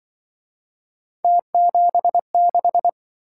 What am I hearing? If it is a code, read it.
Morse code, heard as a keyed tone, about 24 wpm, 700 Hz. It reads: T76
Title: Speed-Racing - Three Letter-Number Combinations - 20wpm